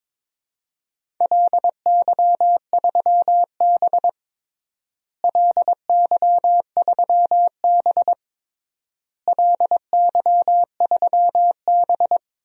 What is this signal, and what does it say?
Morse code, heard as a keyed tone, about 21 wpm, 700 Hz. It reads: LY3B LY3B LY3B
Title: DXCC-100 Call Signs - 4 Characters - Repeated 3x - 22wpm